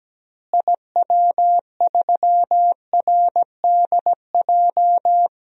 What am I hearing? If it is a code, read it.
Morse code, heard as a keyed tone, about 17 wpm, 700 Hz. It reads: IW3RDJ
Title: DXCC-100 Call Signs - 6 Characters - Repeated 2x - 17wpm